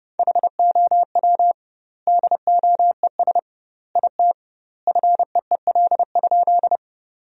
Morse code, heard as a keyed tone, about 32 wpm, 700 Hz. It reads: HOW DOES IT FEEL?
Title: 2x Word-Spacing - Sentences from Top 400 Words - 30wpm